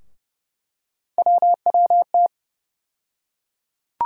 Morse code, heard as a keyed tone, about 30 wpm, 700 Hz. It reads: WWT
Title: USA Call Sign Postfixes - 3 Letters 30wpm